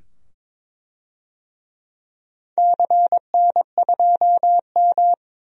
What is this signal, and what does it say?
Morse code, heard as a keyed tone, about 22 wpm, 700 Hz. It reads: CN2M
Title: DXCC-100 Call Signs - 4 Characters - 22wpm